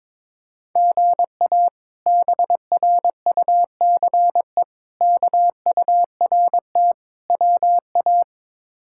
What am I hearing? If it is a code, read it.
Morse code, heard as a keyed tone, about 21 wpm, 700 Hz. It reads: GA BRUCE KURT WA
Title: K1USN SST QSO Practice - 22wpm